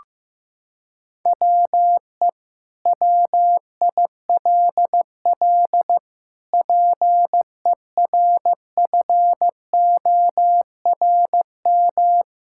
Morse code, heard as a keyed tone, about 15 wpm, 700 Hz. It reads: WE WILL PERFORM
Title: Sentences from Top 1000 Words - Review Part 5 - 15wpm